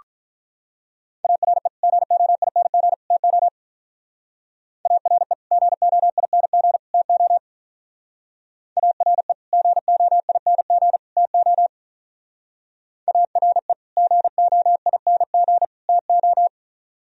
Morse code, heard as a keyed tone, about 44 wpm, 700 Hz. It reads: ARE GOING TO ARE GOING TO ARE GOING TO ARE GOING TO